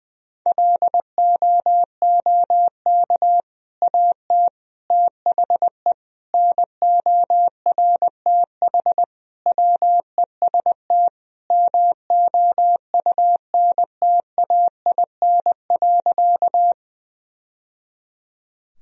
Morse code, heard as a keyed tone, about 20 wpm, 700 Hz. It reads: LOOK AT THE NORTH WEST MOUNTAIN.
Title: Sentences from Top 1000 Words - Review Part 2 - 20wpm